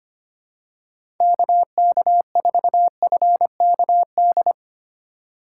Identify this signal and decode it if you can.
Morse code, heard as a keyed tone, about 25 wpm, 700 Hz. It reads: KK4FKD
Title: USA Call Signs - 2x Character Spacing - 25wpm